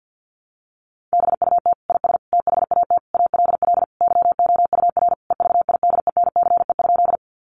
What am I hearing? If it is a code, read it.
Morse code, heard as a keyed tone, about 50 wpm, 700 Hz. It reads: BUT IS THAT ALL YOUR EVIDENCE?
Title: Sentences from Top 1300 Words 50wpm